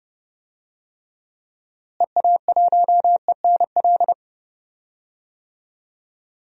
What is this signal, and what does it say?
Morse code, heard as a keyed tone, about 30 wpm, 700 Hz. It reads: EA1ENL
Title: DXCC-100 Call Signs - 6 Characters - Repeated 3x - 30wpm